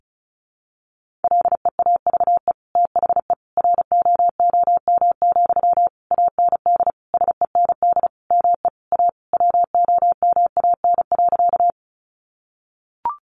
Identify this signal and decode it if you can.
Morse code, heard as a keyed tone, about 34 wpm, 700 Hz. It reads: LEAVE THE ROOM, AND SEND ME A WOMAN.
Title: Sentences from Top 1500 Words 35wpm